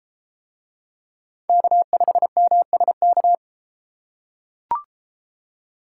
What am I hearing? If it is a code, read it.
Morse code, heard as a keyed tone, about 32 wpm, 700 Hz. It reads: K5MSK